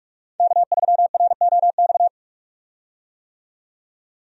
Morse code, heard as a keyed tone, about 46 wpm, 700 Hz. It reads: K3ROX
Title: USA Call Signs - 5 Characters - 45wpm